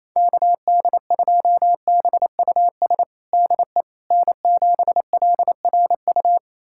Morse code, heard as a keyed tone, about 28 wpm, 700 Hz. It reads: KD2BUS DE N7LRU
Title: USA Callsign DE Callsign 28wpm